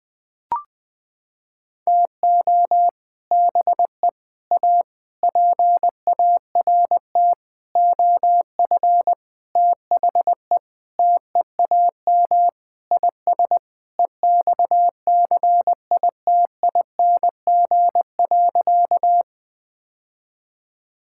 Morse code, heard as a keyed tone, about 21 wpm, 700 Hz. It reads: TO BE A PART OF THE TEAM IS EXCITING.
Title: Sentences with 6-Word Phrases - 20wpm